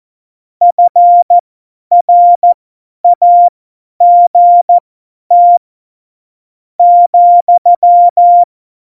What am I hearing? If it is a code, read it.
Morse code, heard as a keyed tone, about 14 wpm, spaced as for 10 wpm, 700 Hz. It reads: FRAGT ,
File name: Morsekurs / Witze / Witz 27 March 2018, 50 F70